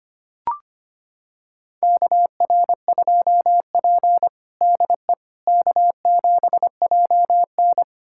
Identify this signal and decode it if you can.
Morse code, heard as a keyed tone, about 25 wpm, 700 Hz. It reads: KR2P DE K7JN
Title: USA Callsign DE Callsign - 4 Characters - 25wpm